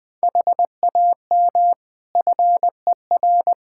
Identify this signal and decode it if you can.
Morse code, heard as a keyed tone, about 20 wpm, 700 Hz. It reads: HAM FER
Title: Practice QSO - Exchange 3 - More Info - 20wpm